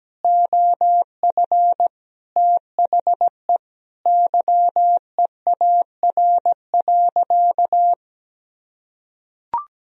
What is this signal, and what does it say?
Morse code, heard as a keyed tone, about 17 wpm, 700 Hz. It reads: OF THE YEAR.
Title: Sentences from Top 1000 Words - Review Part 1 - 17wpm